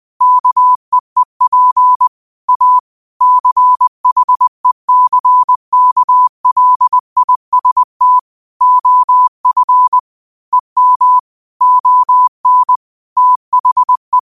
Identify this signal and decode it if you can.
Morse code, heard as a keyed tone, about 20 wpm, 1000 Hz. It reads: KEEP A CHECKLIST OF EM ON THE